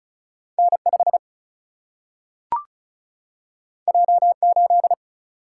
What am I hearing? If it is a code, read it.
Morse code, heard as a keyed tone, about 36 wpm, 700 Hz. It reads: N5 J8
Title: Single-Letter Single-Number 35wpm